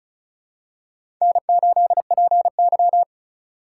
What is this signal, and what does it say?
Morse code, heard as a keyed tone, about 36 wpm, 700 Hz. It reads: N8PY
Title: USA Call Signs - 4 Characters - 3x Character Spacing - 35wpm